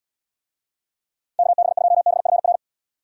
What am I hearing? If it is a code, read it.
Morse code, heard as a keyed tone, about 78 wpm, 700 Hz. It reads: KB2BFR